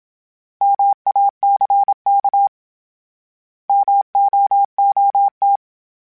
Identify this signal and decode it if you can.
Morse code, heard as a keyed tone, about 27 wpm, 800 Hz. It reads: MACK MOOT